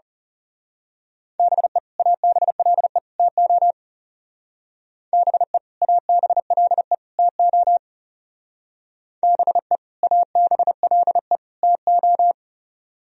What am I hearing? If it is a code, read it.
Morse code, heard as a keyed tone, about 35 wpm, 700 Hz. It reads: BE ABLE TO BE ABLE TO BE ABLE TO
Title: Speed-Racing - Sets of 3 Words 30wpm